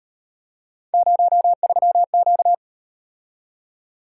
Morse code, heard as a keyed tone, about 38 wpm, 700 Hz. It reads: 03Q